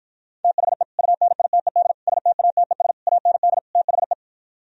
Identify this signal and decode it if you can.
Morse code, heard as a keyed tone, about 52 wpm, 700 Hz. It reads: THE UNITED STATES AND THE